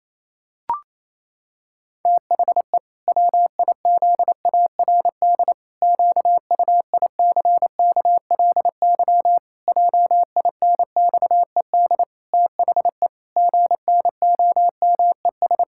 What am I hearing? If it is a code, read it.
Morse code, heard as a keyed tone, about 28 wpm, 700 Hz. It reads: THE WIZARD QUICKLY JINXED THE GNOMES
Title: Pangram Challenge v2 - 28wpm